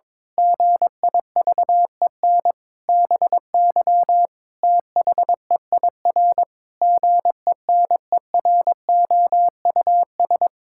Morse code, heard as a keyed tone, about 21 wpm, 700 Hz. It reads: GIVEN BY THEIR GENEROUS
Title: Pangram Challenge v2 - 22wpm